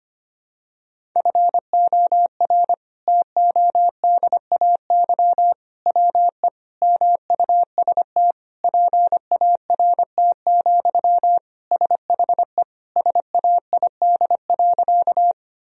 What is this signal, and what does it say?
Morse code, heard as a keyed tone, about 25 wpm, 700 Hz. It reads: FOR TODAY WE MUST PART, SHE SAID.
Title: Sentences from Top 1000 Words - Review Part 1 - 25wpm